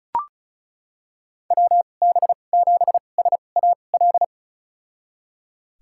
Morse code, heard as a keyed tone, about 34 wpm, spaced as for 25 wpm, 700 Hz. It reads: WB7SAL
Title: USA Call Signs - 6 Characters - 2x Character Spacing - 35wpm